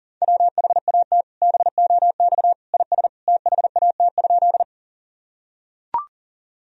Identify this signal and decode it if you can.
Morse code, heard as a keyed tone, about 40 wpm, 700 Hz. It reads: WHAT BOX IS THAT?
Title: Sentences from Top 1000 Words - Review Part 2 - 40wpm